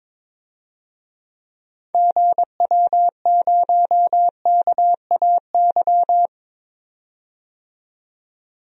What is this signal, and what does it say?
Morse code, heard as a keyed tone, about 22 wpm, 700 Hz. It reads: GW0KAY
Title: DXCC-100 Call Signs - 6 Characters - 22wpm